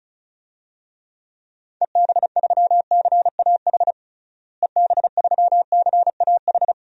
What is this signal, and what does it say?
Morse code, heard as a keyed tone, about 34 wpm, 700 Hz. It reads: EB3CAH EB3CAH
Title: DXCC-100 Call Signs - 6 Characters - Repeated 2x - 35wpm